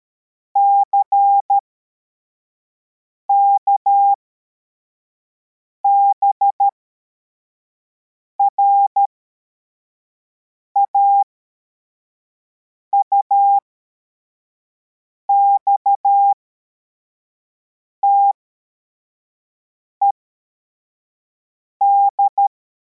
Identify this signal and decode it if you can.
Morse code, heard as a keyed tone, about 13 wpm, spaced as for 4 wpm, 800 Hz. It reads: CKBRA UXTED
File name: formation / cw / Text44